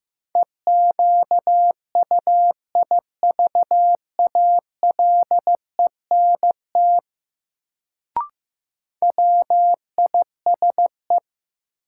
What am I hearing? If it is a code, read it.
Morse code, heard as a keyed tone, about 15 wpm, 700 Hz. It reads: EQUIVALENT WISE
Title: Sentences from Top 1600 Words - New Word Drill Alt - 15wpm